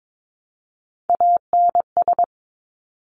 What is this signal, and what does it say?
Morse code, heard as a keyed tone, about 22 wpm, 700 Hz. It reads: ANS
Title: Top Three Letter Combinations in Words 22wpm